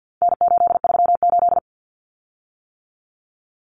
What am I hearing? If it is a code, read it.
Morse code, heard as a keyed tone, about 49 wpm, 700 Hz. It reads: N838
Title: Common Cut Numbers - 4-Digit 50wpm